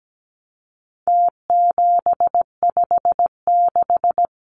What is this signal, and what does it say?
Morse code, heard as a keyed tone, about 17 wpm, 700 Hz. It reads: T756
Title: Numbers - Contest Style - 4-Digit 17wpm